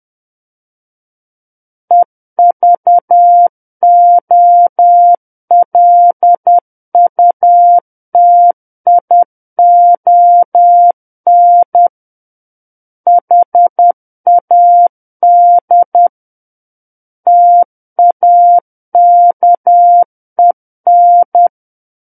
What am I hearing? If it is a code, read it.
Morse code, heard as a keyed tone, about 10 wpm, 700 Hz. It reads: EVOLUTION HAD TAKEN
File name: Book One / 01-Chapter 1 - 10 WPM